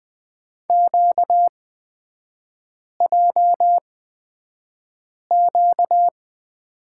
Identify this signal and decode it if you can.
Morse code, heard as a keyed tone, about 20 wpm, spaced as for 5 wpm, 690 Hz. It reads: QJQ
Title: 236 Lesson 38e - Letter q - Three Characters 20wpm